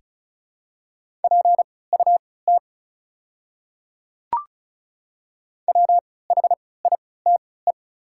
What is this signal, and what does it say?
Morse code, heard as a keyed tone, about 34 wpm, spaced as for 20 wpm, 690 Hz. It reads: PUT WHITE
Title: 3x Character Spacing - Top 500 Words - 35wpm